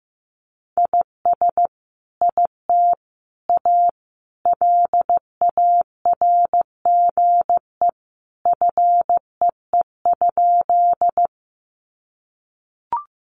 Morse code, heard as a keyed tone, about 15 wpm, 700 Hz. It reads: IS IT A LARGE FEE?
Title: Sentences from Top 1500 Words 15wpm